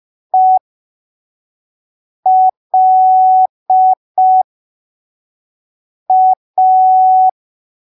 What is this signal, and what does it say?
Morse code, heard as a keyed tone, about 5 wpm, spaced as for 3 wpm, 750 Hz. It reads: ELA
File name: arquivos / texto / texto 20 05ppm